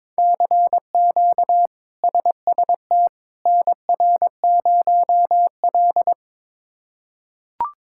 Morse code, heard as a keyed tone, about 22 wpm, 700 Hz. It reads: CQ SST NR0L
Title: K1USN SST QSO Practice - 22wpm